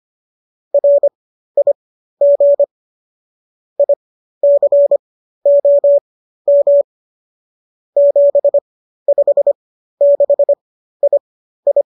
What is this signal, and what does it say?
Morse code, heard as a keyed tone, about 25 wpm, spaced as for 13 wpm, 570 Hz. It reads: RIG ICOM 756II